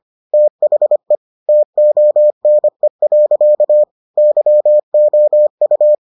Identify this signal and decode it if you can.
Morse code, heard as a keyed tone, about 25 wpm, 600 Hz. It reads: THE TONE. YOU